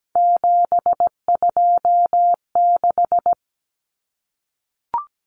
Morse code, heard as a keyed tone, about 17 wpm, 690 Hz. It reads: 726
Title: Common Cut Numbers - 3-Digit 17wpm